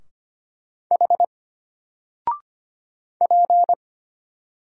Morse code, heard as a keyed tone, about 25 wpm, 700 Hz. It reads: H P
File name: Single-Letter-Number-25wpm